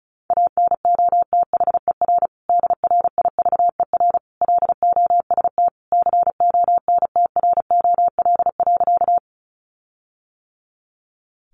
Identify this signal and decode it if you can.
Morse code, heard as a keyed tone, about 34 wpm, 700 Hz. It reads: ANOTHER DRIVER LOST CONTROL.
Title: Sentences from Top 1400 Words 35wpm